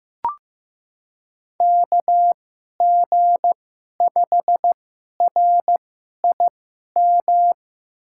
Morse code, heard as a keyed tone, about 15 wpm, spaced as for 11 wpm, 700 Hz. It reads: KG5RIM
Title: USA Call Signs - 6 Characters - 2x Character Spacing - 15wpm